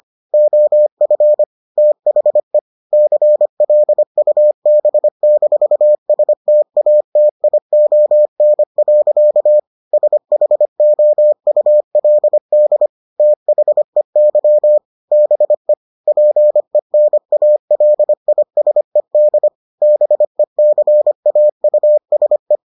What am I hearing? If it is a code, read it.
Morse code, heard as a keyed tone, about 25 wpm, 600 Hz. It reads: OF THE CLUB-STATION. SHOULD THEY BE PENALISED BECAUSE